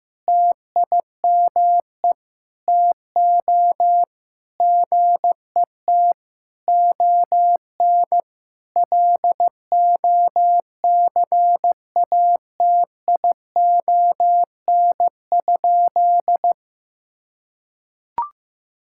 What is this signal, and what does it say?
Morse code, heard as a keyed tone, about 15 wpm, 700 Hz. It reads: TIME TO GET ON LOCATION?
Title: Sentences from Top 1000 Words - Review Part 6 - 15wpm